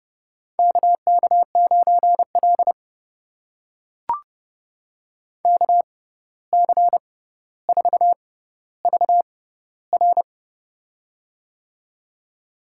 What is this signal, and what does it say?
Morse code, heard as a keyed tone, about 28 wpm, spaced as for 14 wpm, 700 Hz. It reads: KK9L KC4VR
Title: USA Call Signs - 6x Character Spacing - 30wpm